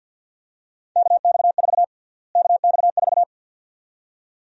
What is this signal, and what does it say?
Morse code, heard as a keyed tone, about 49 wpm, 700 Hz. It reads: KX4 KX4
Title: USA Call Sign Prefixes - Repeated 2x - 50wpm